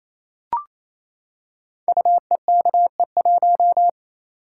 Morse code, heard as a keyed tone, about 27 wpm, 700 Hz. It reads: UEKE1
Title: Code Groups - 5 Letter:Numbers - 28wpm